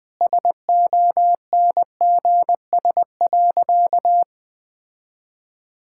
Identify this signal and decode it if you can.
Morse code, heard as a keyed tone, about 20 wpm, 700 Hz. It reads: SONGS.
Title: Sentences with 4-Word Phrases - 20wpm